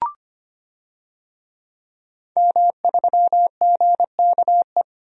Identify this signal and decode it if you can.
Morse code, heard as a keyed tone, about 25 wpm, 700 Hz. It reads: M3GKE
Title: DXCC-100 Call Signs - 5 Characters - 25wpm